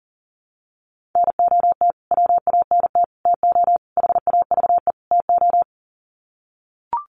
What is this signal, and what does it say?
Morse code, heard as a keyed tone, about 37 wpm, 700 Hz. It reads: NOT WANT TO HAVE TO
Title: Sets of 5 Words Encore2 40wpm